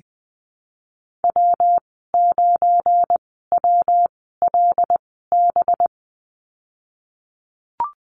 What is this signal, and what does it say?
Morse code, heard as a keyed tone, about 19 wpm, spaced as for 14 wpm, 700 Hz. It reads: W9WLB
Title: 186 Lesson 30g - Letter b - Call Signs 20wpm